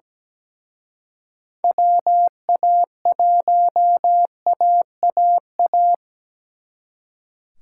Morse code, heard as a keyed tone, about 20 wpm, 700 Hz. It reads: WA1AAA
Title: Speed Racing - USA 2x3 Call Signs 17wpm